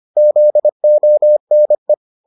Morse code, heard as a keyed tone, about 25 wpm, 600 Hz. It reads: ZONE